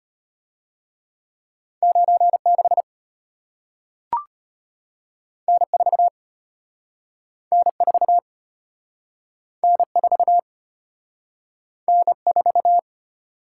Morse code, heard as a keyed tone, about 33 wpm, 700 Hz. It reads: 9 6 N 4 N 4 N 4 N 4